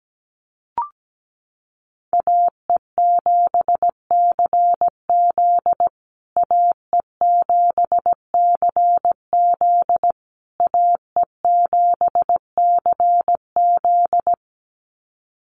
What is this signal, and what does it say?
Morse code, heard as a keyed tone, about 17 wpm, 700 Hz. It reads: AE7CZ AE7CZ AE7CZ
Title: USA Call Signs Repeated 3x 17wpm